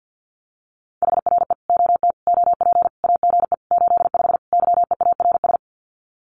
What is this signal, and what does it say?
Morse code, heard as a keyed tone, about 50 wpm, 700 Hz. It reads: HRE OT OP AGE 85 YEARS
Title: Common QSO Phrases 50wpm